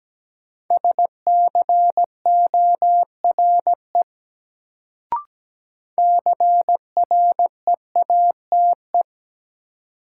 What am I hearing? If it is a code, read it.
Morse code, heard as a keyed tone, about 17 wpm, 700 Hz. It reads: SCORE CREATE
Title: Sentences from Top 1600 Words - New Word Drill Alt - 17wpm